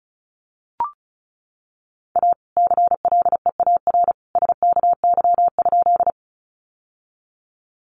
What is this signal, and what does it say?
Morse code, heard as a keyed tone, about 34 wpm, 700 Hz. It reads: A CLEAR SKY?
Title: Sentences from Top 1200 Words 35wpm